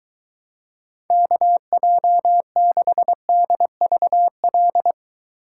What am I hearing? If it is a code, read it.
Morse code, heard as a keyed tone, about 23 wpm, 700 Hz. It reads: KJ6DVL